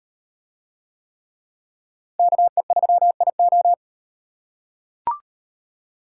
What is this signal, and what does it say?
Morse code, heard as a keyed tone, about 34 wpm, 700 Hz. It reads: KE3IO